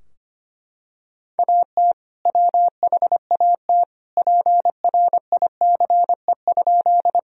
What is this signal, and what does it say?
Morse code, heard as a keyed tone, about 25 wpm, 700 Hz. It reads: AT WHAT PRICE?
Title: Sentences from Top 1000 Words - Review Part 4 - 25wpm